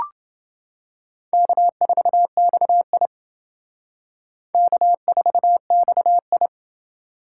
Morse code, heard as a keyed tone, about 28 wpm, 700 Hz. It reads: K4XI K4XI